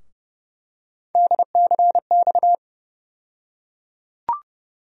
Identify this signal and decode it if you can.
Morse code, heard as a keyed tone, about 30 wpm, 700 Hz. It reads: DCX
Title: USA Call Sign Postfixes - 3 Letters 30wpm